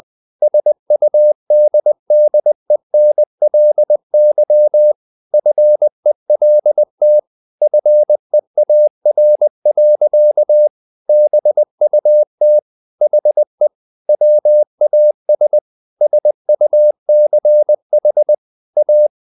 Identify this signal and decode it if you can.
Morse code, heard as a keyed tone, about 20 wpm, 600 Hz. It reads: SUDDENLY FELT FEAR. BUT HE WAS SUCH A
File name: the old man and the sea part 2 20wpm